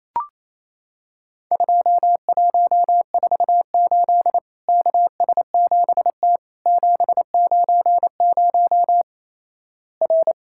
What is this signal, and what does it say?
Morse code, heard as a keyed tone, about 28 wpm, 700 Hz. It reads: 2148 KS7T 790 R
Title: Worked All Europe DX Contest - QTC Exchange - 28wpm